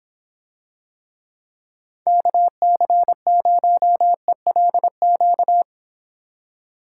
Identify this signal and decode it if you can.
Morse code, heard as a keyed tone, about 26 wpm, 700 Hz. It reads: KC0ELQ